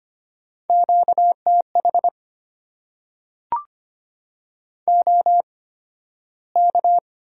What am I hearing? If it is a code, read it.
Morse code, heard as a keyed tone, about 25 wpm, spaced as for 13 wpm, 700 Hz. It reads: QTH OK
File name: Top 90 QSO Elements - 8x Character Spacing - 25wpm